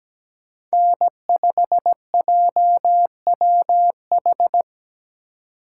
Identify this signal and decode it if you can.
Morse code, heard as a keyed tone, about 17 wpm, 700 Hz. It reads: N5JWH
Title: USA Call Signs - 3x Character Spacing - 17wpm